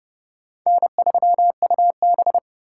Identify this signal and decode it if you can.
Morse code, heard as a keyed tone, about 32 wpm, 700 Hz. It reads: N3UB
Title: USA Call Signs - 4 Characters - 2x Character Spacing - 30wpm